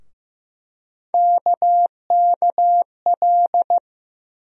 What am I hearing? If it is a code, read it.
Morse code, heard as a keyed tone, about 15 wpm, 700 Hz. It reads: KKL
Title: USA Call Sign Postfixes - 3 Letters 15wpm